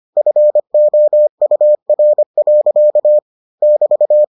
Morse code, heard as a keyed tone, about 25 wpm, 600 Hz. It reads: FOUR. <BT>